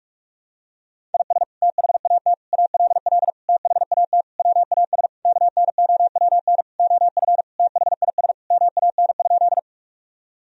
Morse code, heard as a keyed tone, about 46 wpm, 700 Hz. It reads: IS THAT ALL THAT WAS KNOWN OF THIS MAN?
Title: Sentences from Top 1000 Words - Review Part 3 - 45wpm